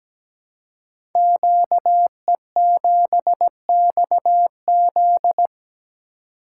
Morse code, heard as a keyed tone, about 17 wpm, 700 Hz. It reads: QE7XZ
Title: Code Groups - 5 Letter:Numbers - 17wpm